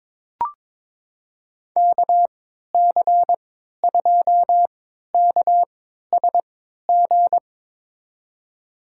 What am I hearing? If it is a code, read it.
Morse code, heard as a keyed tone, about 22 wpm, spaced as for 13 wpm, 700 Hz. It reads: KC2KSG
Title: USA Call Signs - 6 Characters - 3x Character Spacing - 22wpm